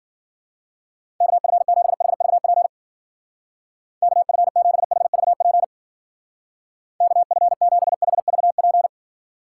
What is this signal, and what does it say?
Morse code, heard as a keyed tone, about 51 wpm, 700 Hz. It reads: KF7HVP KF7HVP KF7HVP